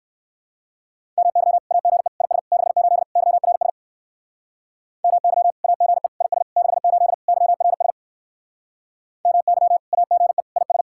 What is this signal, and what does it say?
Morse code, heard as a keyed tone, about 58 wpm, 700 Hz. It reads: MY AGE IS 68 YRS MY AGE IS 68 YRS MY AGE IS